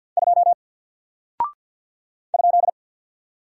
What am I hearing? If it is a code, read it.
Morse code, heard as a keyed tone, about 48 wpm, 700 Hz. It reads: 2 ?
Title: Single Character (Letter:Number:Punctuation:Prosign) Alt 50wpm